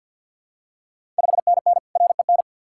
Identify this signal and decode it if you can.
Morse code, heard as a keyed tone, about 51 wpm, 690 Hz. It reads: 5NN REN
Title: Ontario QSO Party Exchange 50wpm